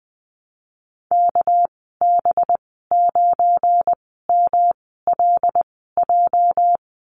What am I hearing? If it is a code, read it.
Morse code, heard as a keyed tone, about 20 wpm, spaced as for 14 wpm, 700 Hz. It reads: KB9MLJ
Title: 207 Lesson 33g - Letter j - Call Signs 20wpm